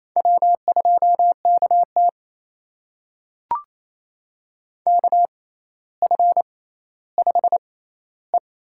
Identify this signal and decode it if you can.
Morse code, heard as a keyed tone, about 28 wpm, spaced as for 14 wpm, 700 Hz. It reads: W2KT KF5E
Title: USA Call Signs - 4 Characters - 6x Character Spacing - 28wpm